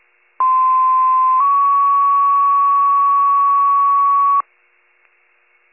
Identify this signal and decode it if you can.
Two-tone page, heard as a one-second tone, then a three-second tone, about 1040 then 1120 Hz. It is heard from 0.4 to 4.4 s.